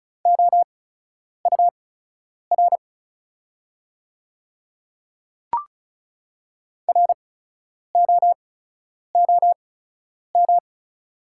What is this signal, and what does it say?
Morse code, heard as a keyed tone, about 36 wpm, spaced as for 10 wpm, 690 Hz. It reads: OUR ROOM
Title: 8x Character Spacing - Top 300 Words - 35wpm